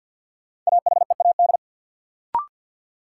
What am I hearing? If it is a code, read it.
Morse code, heard as a keyed tone, about 49 wpm, 700 Hz. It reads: AHEAD
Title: Sentences from Top 1000 Words - Review Part 6 - New Word Drill - 50wpm